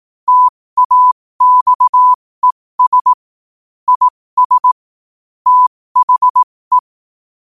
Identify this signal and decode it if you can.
Morse code, heard as a keyed tone, about 18 wpm, spaced as for 15 wpm, 1000 Hz. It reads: TAXES IS THE
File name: qotdfiles / 15wpm